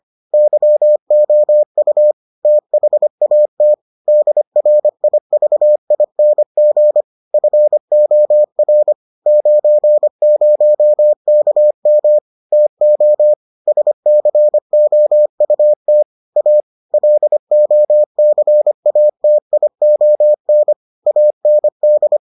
Morse code, heard as a keyed tone, about 25 wpm, 600 Hz. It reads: YOU THAT DRIVING FOR 90KM TO SCOUT A LOCATION AND